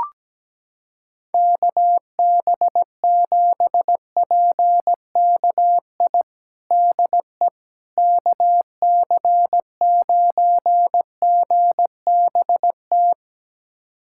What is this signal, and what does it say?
Morse code, heard as a keyed tone, about 17 wpm, 700 Hz. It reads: KB7PKI DE KC9GBT
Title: USA Callsign DE Callsign - 6 Characters - 17wpm